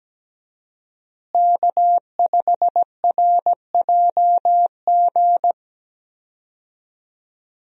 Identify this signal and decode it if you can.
Morse code, heard as a keyed tone, about 17 wpm, 700 Hz. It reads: K5RJG
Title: USA Call Signs - 5 Characters - Repeated 3x - 17wpm